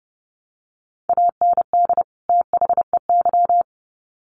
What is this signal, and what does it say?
Morse code, heard as a keyed tone, about 30 wpm, 700 Hz. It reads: AND THEY
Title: Sets of 2 Words 30wpm